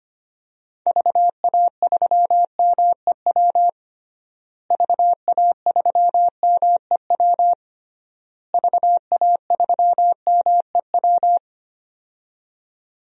Morse code, heard as a keyed tone, about 25 wpm, 700 Hz. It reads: VA3MEW VA3MEW VA3MEW
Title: DXCC-100 Call Signs - 6 Characters - Repeated 3x - 25wpm